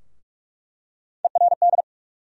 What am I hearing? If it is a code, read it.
Morse code, heard as a keyed tone, about 44 wpm, 700 Hz. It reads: ERD
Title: USA Call Sign Postfixes - 3 Letters 45wpm